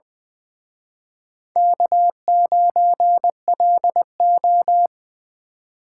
Morse code, heard as a keyed tone, about 24 wpm, 700 Hz. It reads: K9LO
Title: Speed Racing - USA 1x2 Call Signs 20wpm